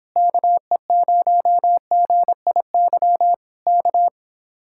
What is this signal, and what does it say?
Morse code, heard as a keyed tone, about 26 wpm, 700 Hz. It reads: KE0GIY K